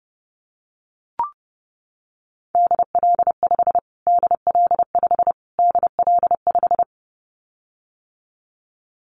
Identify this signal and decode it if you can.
Morse code, heard as a keyed tone, about 30 wpm, 700 Hz. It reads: DL5 DL5 DL5
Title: Callsign Prefixes for Top 25 DXCC - Repeated 3x - 30wpm